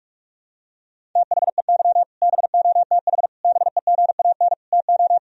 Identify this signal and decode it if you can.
Morse code, heard as a keyed tone, about 46 wpm, 700 Hz. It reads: THEY BOTH BEGAN TO
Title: Sentences from Top 1000 Words - Review Part 1 - 45wpm